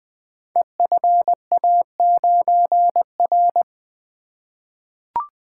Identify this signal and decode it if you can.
Morse code, heard as a keyed tone, about 20 wpm, 700 Hz. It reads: EFA9R
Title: Code Groups - 5 Letter:Numbers - 20wpm